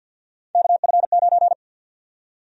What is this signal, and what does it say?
Morse code, heard as a keyed tone, about 52 wpm, 700 Hz. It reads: KF9
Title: USA Call Sign Prefixes - 50wpm